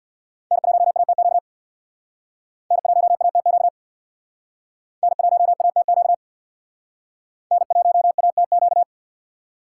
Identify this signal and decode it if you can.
Morse code, heard as a keyed tone, about 61 wpm, 700 Hz. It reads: N1ATQ N1ATQ N1ATQ N1ATQ
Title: Speed Racing - USA Call Signs 50wpm